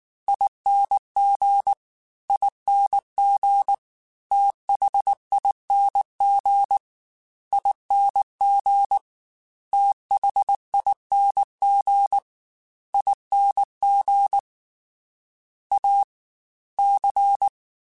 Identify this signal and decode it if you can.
Morse code, heard as a keyed tone, about 19 wpm, 800 Hz. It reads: ING ING THING ING THING ING A C